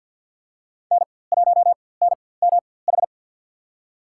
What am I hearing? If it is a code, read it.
Morse code, heard as a keyed tone, about 47 wpm, spaced as for 23 wpm, 700 Hz. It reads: N1NMH
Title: USA Call Signs - 5 Characters - 4x Character Spacing - 50wpm